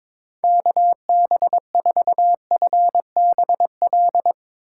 Morse code, heard as a keyed tone, about 22 wpm, 700 Hz. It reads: KB4FBL
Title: USA Call Signs - 8x Character Spacing - 22wpm